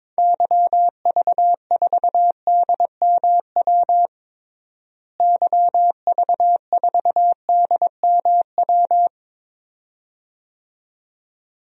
Morse code, heard as a keyed tone, about 21 wpm, 700 Hz. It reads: YV4DMW YV4DMW
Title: DXCC-100 Call Signs - 6 Characters - Repeated 2x - 22wpm